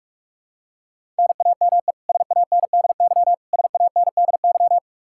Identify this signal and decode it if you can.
Morse code, heard as a keyed tone, about 46 wpm, 700 Hz. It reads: NAME SANDY SANDY
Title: K1USN SST QSO Practice 45wpm